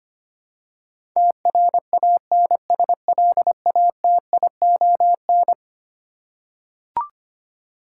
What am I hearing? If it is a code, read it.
Morse code, heard as a keyed tone, about 25 wpm, 700 Hz. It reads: TRANSLATION
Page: morse-code-ninja-practice-25wpm